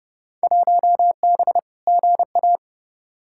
Morse code, heard as a keyed tone, about 30 wpm, 700 Hz. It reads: 1B GA
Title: Field Day Exchanges Encore - 30wpm